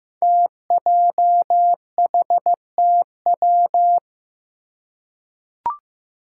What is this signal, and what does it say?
Morse code, heard as a keyed tone, about 15 wpm, 700 Hz. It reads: TJHTW
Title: Five Challenging Letter-Number Combinations 15wpm